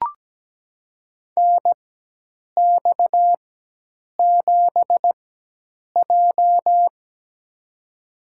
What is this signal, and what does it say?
Morse code, heard as a keyed tone, about 17 wpm, spaced as for 8 wpm, 700 Hz. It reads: NX7J
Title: USA Call Signs - 4 Characters - 4x Character Spacing - 17wpm